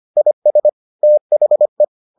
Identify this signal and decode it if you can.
Morse code, heard as a keyed tone, about 25 wpm, 600 Hz. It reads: IS THE